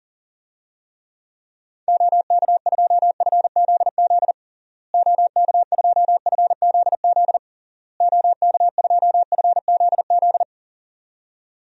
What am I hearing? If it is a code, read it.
Morse code, heard as a keyed tone, about 40 wpm, 700 Hz. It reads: OK2FZZ OK2FZZ OK2FZZ
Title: DXCC-100 Call Signs - 6 Characters - Repeated 3x - 40wpm